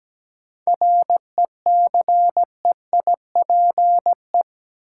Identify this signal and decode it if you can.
Morse code, heard as a keyed tone, about 17 wpm, 700 Hz. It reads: RECEIPE
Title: Sentences from Top 1200 Words - New Word Drill Alt - 17wpm